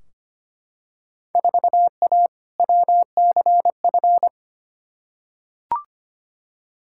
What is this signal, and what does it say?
Morse code, heard as a keyed tone, about 25 wpm, 700 Hz. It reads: 4A WCF
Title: Field Day Exchanges - 25wpm